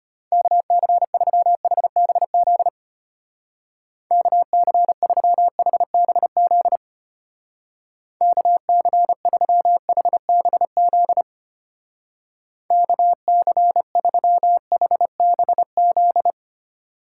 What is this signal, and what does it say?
Morse code, heard as a keyed tone, about 32 wpm, 700 Hz. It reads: KC3HBZ KC3HBZ KC3HBZ KC3HBZ